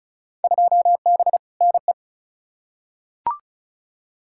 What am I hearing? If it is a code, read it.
Morse code, heard as a keyed tone, about 36 wpm, 690 Hz. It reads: 2B NE
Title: Field Day Exchanges Encore - 35wpm